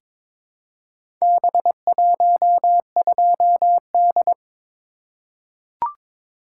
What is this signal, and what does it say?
Morse code, heard as a keyed tone, about 22 wpm, 700 Hz. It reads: B12D
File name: practice / 22wpm / Four Letter-Number Combinations - 22wpm